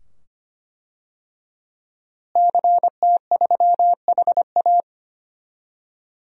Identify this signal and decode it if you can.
Morse code, heard as a keyed tone, about 25 wpm, 700 Hz. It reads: CT3HA
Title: DXCC-100 Call Signs - 5 Characters - Repeated 2x - 25wpm